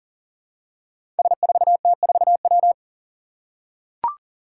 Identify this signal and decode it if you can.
Morse code, heard as a keyed tone, about 38 wpm, 700 Hz. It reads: S4T4W
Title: Five Challenging Letter-Number Combinations 40wpm